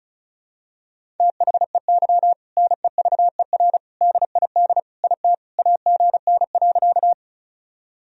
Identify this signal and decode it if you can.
Morse code, heard as a keyed tone, about 36 wpm, 700 Hz. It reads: THEY NEVER DID IT AGN.
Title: Sentences from Top 300 Words and Abbreviations 35wpm